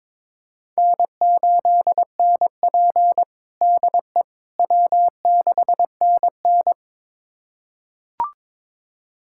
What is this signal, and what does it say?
Morse code, heard as a keyed tone, about 21 wpm, 700 Hz. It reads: N8NP DE W6NN
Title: USA Callsign DE Callsign - 4 Characters - 22wpm